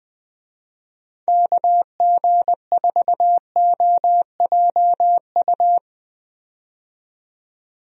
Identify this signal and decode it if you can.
Morse code, heard as a keyed tone, about 20 wpm, 700 Hz. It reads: KG4OJU
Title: USA Call Signs - 6 Characters - 20wpm